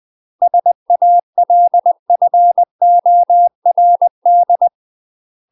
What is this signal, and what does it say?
Morse code, heard as a keyed tone, about 20 wpm, 700 Hz. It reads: SALFORD
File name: UK MASTER QSOs TRAINING / 20WPM / UK MASTER QSOs 0060